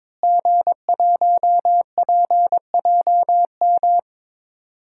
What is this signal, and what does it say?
Morse code, heard as a keyed tone, about 22 wpm, 700 Hz. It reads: G1PJM
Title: DXCC-100 Call Signs - 5 Characters - Repeated 3x - 22wpm